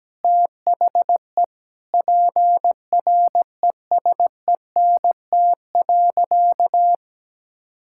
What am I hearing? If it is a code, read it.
Morse code, heard as a keyed tone, about 17 wpm, 700 Hz. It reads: THE PRESENT.
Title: Sentences from Top 1000 Words - Review Part 5 - 17wpm